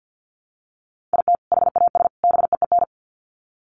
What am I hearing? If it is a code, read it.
Morse code, heard as a keyed tone, about 50 wpm, 700 Hz. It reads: IT HAS BEEN
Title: Sets of 3 Words 50wpm